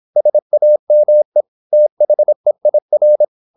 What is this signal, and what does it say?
Morse code, heard as a keyed tone, about 26 wpm, 600 Hz. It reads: SAME THEIR
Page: Never 26wpm | CW med Gnister